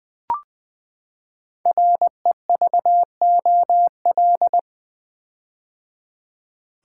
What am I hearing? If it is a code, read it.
Morse code, heard as a keyed tone, about 20 wpm, 700 Hz. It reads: REVOL
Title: Build A Word - 5 Letters At A Time - 10-Letter Words - 20wpm